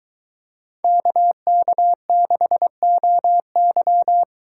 Morse code, heard as a keyed tone, about 22 wpm, 700 Hz. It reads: KK6OY